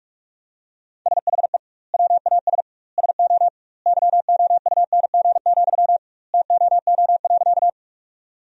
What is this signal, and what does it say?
Morse code, heard as a keyed tone, about 46 wpm, 700 Hz. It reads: SHE WAS SO YOUNG, TOO.
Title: 2x Word-Spacing - Sentences from Top 400 Words - 45wpm